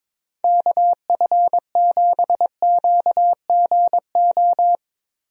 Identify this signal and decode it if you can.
Morse code, heard as a keyed tone, about 22 wpm, 700 Hz. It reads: KF7QGO
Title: USA Call Signs - 6 Characters - Repeated 2x - 22wpm